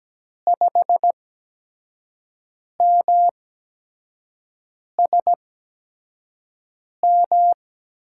Morse code, heard as a keyed tone, about 17 wpm, spaced as for 5 wpm, 700 Hz. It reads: 5MSM
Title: USA Call Signs - 6 Characters - 8x Character Spacing - 17wpm